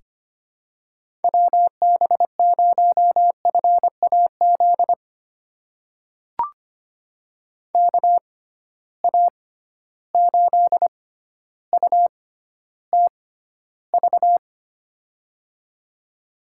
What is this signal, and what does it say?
Morse code, heard as a keyed tone, about 25 wpm, spaced as for 13 wpm, 700 Hz. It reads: WB0FAZ KA8UTV
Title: USA Call Signs - 6 Characters - 6x Character Spacing - 25wpm